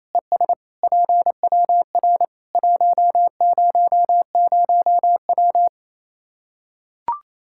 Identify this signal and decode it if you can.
Morse code, heard as a keyed tone, about 28 wpm, 700 Hz. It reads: ES PWR 100W
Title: Practice QSO - Exchange 2 - Info - 28wpm